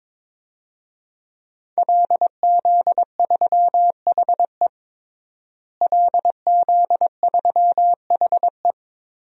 Morse code, heard as a keyed tone, about 22 wpm, 700 Hz. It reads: LZ3HE LZ3HE
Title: DXCC-100 Call Signs - 5 Characters - Repeated 2x - 22wpm